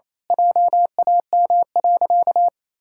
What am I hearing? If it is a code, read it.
Morse code, heard as a keyed tone, about 28 wpm, 700 Hz. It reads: JAM.
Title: Pangram Challenge v2 - 28wpm